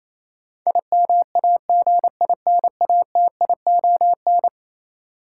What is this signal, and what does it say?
Morse code, heard as a keyed tone, about 27 wpm, 700 Hz. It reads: IMAGINATION
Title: Sentences from Top 1300 Words - New Word Drill - 28wpm